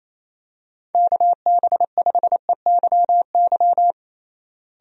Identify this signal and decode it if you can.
Morse code, heard as a keyed tone, about 27 wpm, 700 Hz. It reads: KB5EYY
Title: USA Call Signs - 6 Characters - Repeated 2x - 28wpm